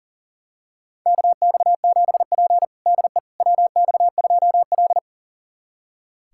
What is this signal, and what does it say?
Morse code, heard as a keyed tone, about 40 wpm, 700 Hz. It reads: KX7P DE WX2L
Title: USA Callsign DE Callsign - 4 Characters - 40wpm